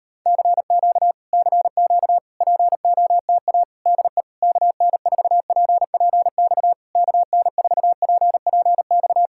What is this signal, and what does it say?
Morse code, heard as a keyed tone, about 36 wpm, 700 Hz. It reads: CQ CQ POTA DE KN4PPX KN4PPX